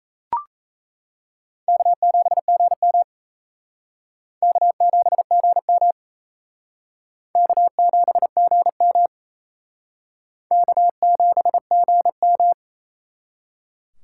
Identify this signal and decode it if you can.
Morse code, heard as a keyed tone, about 35 wpm, 700 Hz. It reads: K7GM K7GM K7GM K7GM